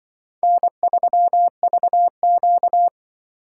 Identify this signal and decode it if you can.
Morse code, heard as a keyed tone, about 24 wpm, 700 Hz. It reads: N3VQ
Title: Speed Racing - USA 1x2 Call Signs 20wpm